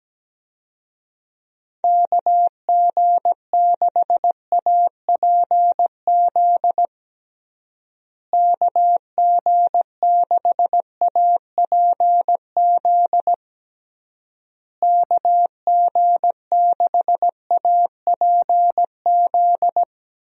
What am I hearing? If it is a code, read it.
Morse code, heard as a keyed tone, about 17 wpm, 700 Hz. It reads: KG6APZ KG6APZ KG6APZ
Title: USA Call Signs - 6 Characters - Repeated 3x - 17wpm